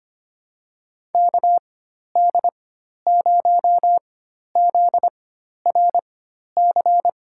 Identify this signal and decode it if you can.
Morse code, heard as a keyed tone, about 25 wpm, spaced as for 12 wpm, 700 Hz. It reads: KD0ZRC
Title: USA Call Signs - 6 Characters - 4x Character Spacing - 25wpm